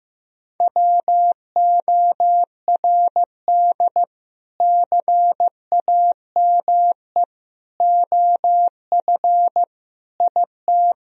Morse code, heard as a keyed tone, about 15 wpm, 700 Hz. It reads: WORD CAME OF IT
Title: Sentences from Top 1000 Words - Review Part 4 - 15wpm